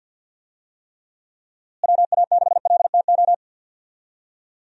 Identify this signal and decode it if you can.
Morse code, heard as a keyed tone, about 50 wpm, 700 Hz. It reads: WA6LTO
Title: USA Call Signs - 6 Characters - Repeated 2x - 50wpm